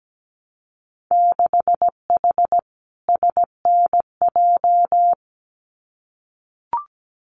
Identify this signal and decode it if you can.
Morse code, heard as a keyed tone, about 17 wpm, 700 Hz. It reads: 6H SNJ
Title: Winter Field Day Exchanges Encore 17wpm